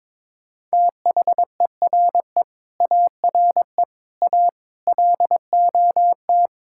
Morse code, heard as a keyed tone, about 22 wpm, 700 Hz. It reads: THERE ARE A LOT
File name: Sentences with 7-Word Phrases - 22wpm